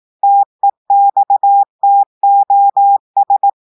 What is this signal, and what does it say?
Morse code, heard as a keyed tone, about 18 wpm, 800 Hz. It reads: TEXTOS
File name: audios / portugais / portugais texte en clair 1 vitesse 18 mots minutes